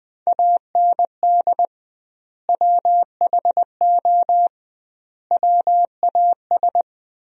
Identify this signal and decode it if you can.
Morse code, heard as a keyed tone, about 21 wpm, 700 Hz. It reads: AND WHO WAS
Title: 2x Word-Spacing - Sentences from Top 100 Words - 20wpm